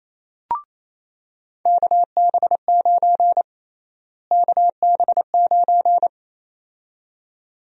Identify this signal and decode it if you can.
Morse code, heard as a keyed tone, about 28 wpm, 700 Hz. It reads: KB9 KB9
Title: USA Call Sign Prefixes - Repeated 2x - 28wpm